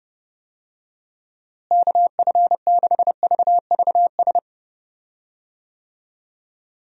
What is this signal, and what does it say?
Morse code, heard as a keyed tone, about 30 wpm, 700 Hz. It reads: KF6VVS
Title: USA Call Signs - 6 Characters - Repeated 3x - 30wpm